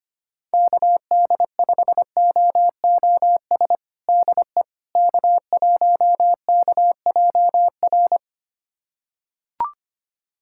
Morse code, heard as a keyed tone, about 25 wpm, 700 Hz. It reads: KD5OOS DE K1KJR
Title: USA Callsign DE Callsign 25wpm